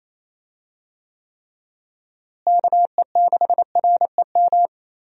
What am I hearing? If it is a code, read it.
Morse code, heard as a keyed tone, about 28 wpm, 700 Hz. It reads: KE6REM
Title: USA Call Signs - 6 Characters - Repeated 2x - 28wpm